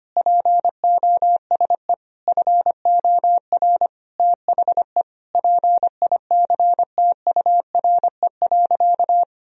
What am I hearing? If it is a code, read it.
Morse code, heard as a keyed tone, about 25 wpm, 700 Hz. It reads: POSE FOR THE PICTURE.
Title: Sentences from Top 1000 Words - Review Part 2 - 25wpm